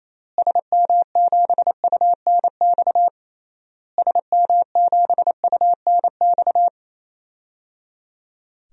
Morse code, heard as a keyed tone, about 27 wpm, 700 Hz. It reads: SM7UNX SM7UNX
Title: DXCC-100 Call Signs - 6 Characters - Repeated 3x - 28wpm